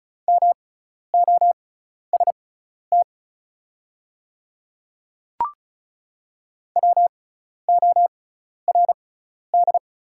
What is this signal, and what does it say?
Morse code, heard as a keyed tone, about 36 wpm, spaced as for 12 wpm, 700 Hz. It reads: MOST WORD
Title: 6x Character Spacing - Top 100 Words - 35wpm